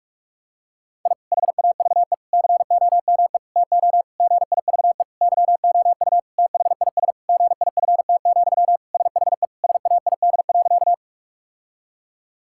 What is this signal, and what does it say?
Morse code, heard as a keyed tone, about 41 wpm, 700 Hz. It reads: I HAVE COME TO GIVE YOU THIS GIFT, SHE SAID.
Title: Sentences from Top 1000 Words - Review Part 6 - 45wpm